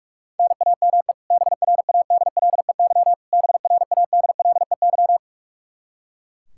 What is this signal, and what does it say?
Morse code, heard as a keyed tone, about 45 wpm, 700 Hz. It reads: NAME BRADLEY BRADLEY
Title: Practice QSO - Exchange 1 - Beginning - 45wpm